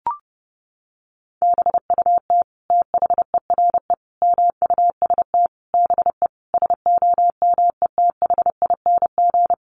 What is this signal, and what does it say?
Morse code, heard as a keyed tone, about 30 wpm, 700 Hz. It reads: BUT THERE MUST BE SOMETHING
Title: Sentences from Top 1500 Words 30wpm